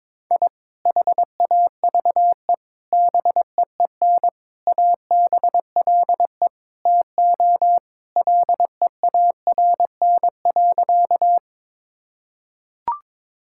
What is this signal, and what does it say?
Morse code, heard as a keyed tone, about 21 wpm, 700 Hz. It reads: I HAVE BEEN ABLE TO LEARN.
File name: practice / 22wpm / Sentences with 3-Word Phrases - 22wpm